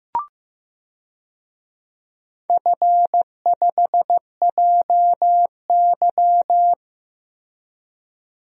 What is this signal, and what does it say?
Morse code, heard as a keyed tone, about 15 wpm, 700 Hz. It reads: F5JY
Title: DXCC-100 Call Signs - 4 Characters - Repeated 2x - 15wpm